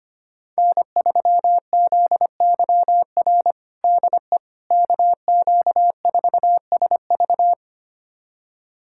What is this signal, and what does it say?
Morse code, heard as a keyed tone, about 25 wpm, 700 Hz. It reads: N3ZYR DE KQ4SV
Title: USA Callsign DE Callsign - 5 Characters - 25wpm